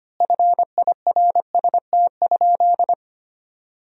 Morse code, heard as a keyed tone, about 25 wpm, 700 Hz. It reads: FIRST?
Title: 4x Word-Spacing - Sentences from Top 100 Words - 25wpm